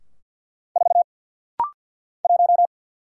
Morse code, heard as a keyed tone, about 50 wpm, 700 Hz. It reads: 4 1
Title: Single Letter-Number - Rapid-Fire 50wpm